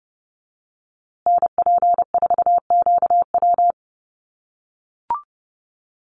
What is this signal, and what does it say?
Morse code, heard as a keyed tone, about 30 wpm, 700 Hz. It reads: NP4QW
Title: USA Call Signs Repeated 3x 30wpm